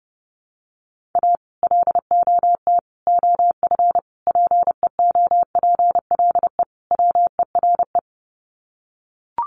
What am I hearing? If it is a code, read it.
Morse code, heard as a keyed tone, about 30 wpm, 700 Hz. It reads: A LOT OF PEOPLE WERE
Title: Sets of 5 Words Encore2 30wpm